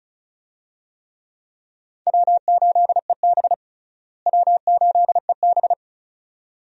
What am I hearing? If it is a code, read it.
Morse code, heard as a keyed tone, about 34 wpm, 700 Hz. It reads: W8EB W8EB
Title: USA Call Signs - 4 Characters - Repeated 2x - 35wpm